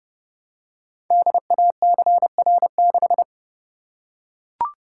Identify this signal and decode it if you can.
Morse code, heard as a keyed tone, about 30 wpm, 700 Hz. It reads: DACR6
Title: Code Groups - 5 Letter:Numbers - 30wpm